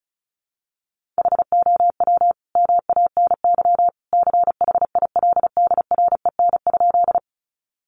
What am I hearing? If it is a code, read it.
Morse code, heard as a keyed tone, about 34 wpm, 700 Hz. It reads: HOW MANY CHILDREN?
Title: Sentences from Top 300 Words 35wpm